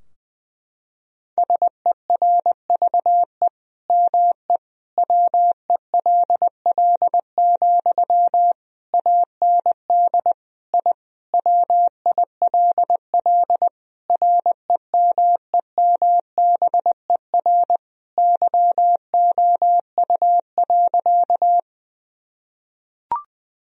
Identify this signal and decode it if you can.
Morse code, heard as a keyed tone, about 20 wpm, 700 Hz. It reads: SERVE ME WELL, AND I WILL REMEMBER YOU.
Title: Sentences from Top 1000 Words - Review Part 2 - 20wpm